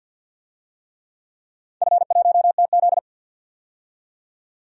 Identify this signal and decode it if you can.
Morse code, heard as a keyed tone, about 51 wpm, 700 Hz. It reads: F1TZ
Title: DXCC-100 Call Signs - 4 Characters - 50wpm